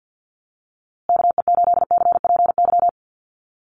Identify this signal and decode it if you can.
Morse code, heard as a keyed tone, about 48 wpm, 700 Hz. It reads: KE8CPY
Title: USA Call Signs Repeated 3x 50wpm